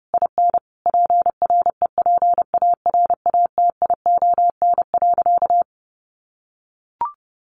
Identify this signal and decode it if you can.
Morse code, heard as a keyed tone, about 30 wpm, 700 Hz. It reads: IN PREPARATION.
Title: Sentences from Top 1400 Words 30wpm